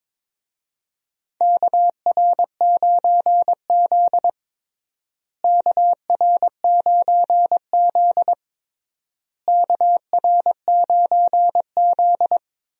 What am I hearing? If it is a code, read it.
Morse code, heard as a keyed tone, about 22 wpm, 700 Hz. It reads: KR9Z KR9Z KR9Z
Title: USA Call Signs - 4 Characters - Repeated 3x - 22wpm